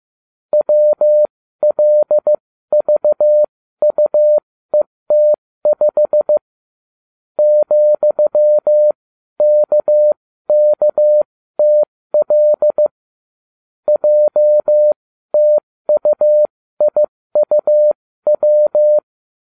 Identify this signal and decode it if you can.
Morse code, heard as a keyed tone, about 15 wpm, spaced as for 13 wpm, 600 Hz. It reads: WLVUET5 ,KKTL JTUIUW